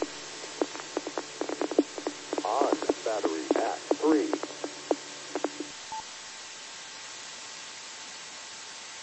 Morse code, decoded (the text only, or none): none